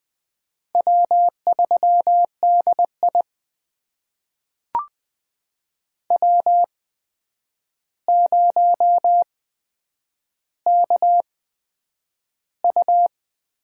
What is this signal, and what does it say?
Morse code, heard as a keyed tone, about 20 wpm, spaced as for 9 wpm, 700 Hz. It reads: W3DI W0KU
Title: USA Call Signs - 4 Characters - 8x Character Spacing - 20wpm